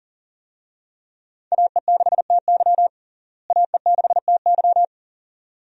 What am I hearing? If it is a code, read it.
Morse code, heard as a keyed tone, about 37 wpm, 700 Hz. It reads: AE6TY AE6TY
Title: USA Call Signs - 5 Characters - Repeated 2x - 40wpm